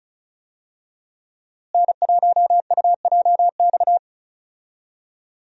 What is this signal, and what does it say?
Morse code, heard as a keyed tone, about 34 wpm, 700 Hz. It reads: N1UJX
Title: USA Call Signs - 5 Characters - Repeated 2x - 35wpm